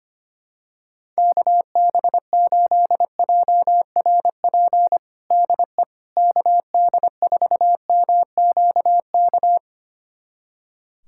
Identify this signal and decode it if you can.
Morse code, heard as a keyed tone, about 25 wpm, 700 Hz. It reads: KB8JRP DE KD4MQK
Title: USA Callsign DE Callsign - 6 Characters - 25wpm